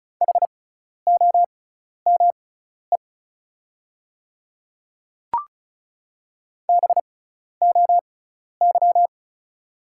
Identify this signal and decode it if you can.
Morse code, heard as a keyed tone, about 34 wpm, spaced as for 12 wpm, 700 Hz. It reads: HOME BOY
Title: 6x Character Spacing - Top 200 Words - 35wpm